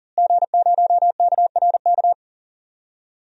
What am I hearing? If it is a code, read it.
Morse code, heard as a keyed tone, about 40 wpm, 700 Hz. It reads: G0KRK
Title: DXCC-100 Call Signs - 5 Characters - Repeated 3x - 40wpm